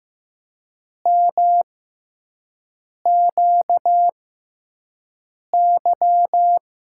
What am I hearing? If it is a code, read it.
Morse code, heard as a keyed tone, about 15 wpm, spaced as for 5 wpm, 700 Hz. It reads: MQY
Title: USA Call Signs - 5 Characters - 6x Character Spacing - 15wpm